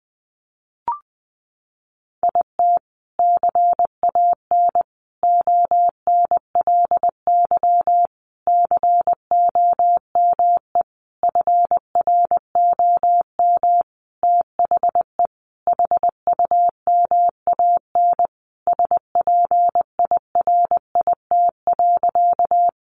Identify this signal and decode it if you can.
Morse code, heard as a keyed tone, about 20 wpm, 700 Hz. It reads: IT CAN ONLY COME FROM THE HUMAN SPIRIT.
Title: Sentences from Top 1400 Words 20wpm